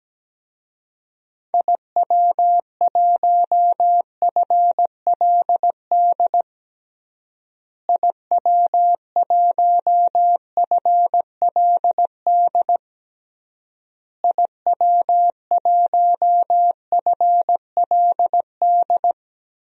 Morse code, heard as a keyed tone, about 17 wpm, 700 Hz. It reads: IW1FLD IW1FLD IW1FLD
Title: DXCC-100 Call Signs - 6 Characters - Repeated 3x - 17wpm